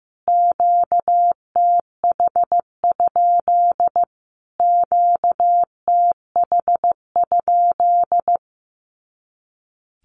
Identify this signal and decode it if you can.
Morse code, heard as a keyed tone, about 15 wpm, 700 Hz. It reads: QTH? QTH?
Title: Q-Codes Repeated 3x 15wpm